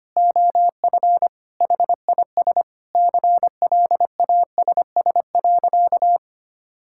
Morse code, heard as a keyed tone, about 25 wpm, 700 Hz. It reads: OF HIS CLASS.
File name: practice / 25wpm / Sentences with 4-Word Phrases - 25wpm